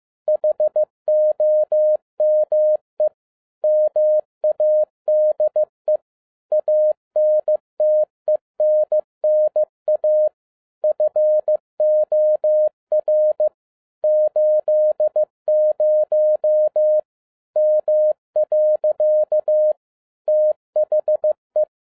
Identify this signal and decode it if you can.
Morse code, heard as a keyed tone, about 15 wpm, 610 Hz. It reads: HOME MADE ANTENNA FOR 80 M. THE
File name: 2012-01-mp3 / n07-ARA JR0DIL-15wpm